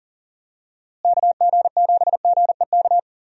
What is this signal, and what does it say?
Morse code, heard as a keyed tone, about 37 wpm, 700 Hz. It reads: KG7GEK
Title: USA Call Signs - 6 Characters - Repeated 2x - 40wpm